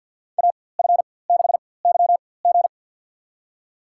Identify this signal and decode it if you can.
Morse code, heard as a keyed tone, about 50 wpm, spaced as for 23 wpm, 700 Hz. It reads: AF6YG
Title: USA Call Signs - 5 Characters - 4x Character Spacing - 50wpm